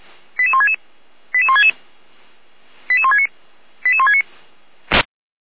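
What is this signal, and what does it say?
Five-tone selective call (ZVEI), heard as quick runs of five tones, tones of about 70 ms. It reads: 88166 88166 88169 88169
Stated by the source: ZVEI five-tone call received by SDR, from the feuerwehr-orlamuende.org fire brigade server.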